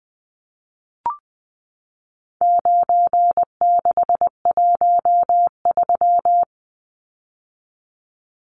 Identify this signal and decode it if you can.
Morse code, heard as a keyed tone, about 20 wpm, 700 Hz. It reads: 9613
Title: Numbers - 4-Digit 20wpm